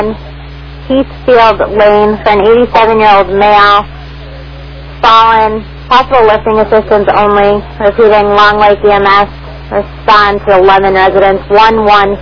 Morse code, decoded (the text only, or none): none